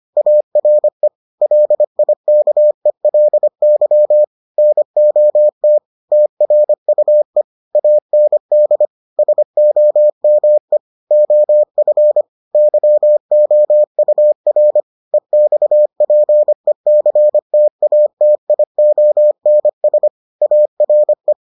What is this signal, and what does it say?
Morse code, heard as a keyed tone, about 25 wpm, 600 Hz. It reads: ARE LIKELY NOT TRUE AND SOME OF YOUR EXPECTATIONS ARE